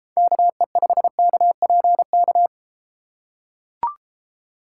Morse code, heard as a keyed tone, about 32 wpm, 700 Hz. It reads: KE5KPK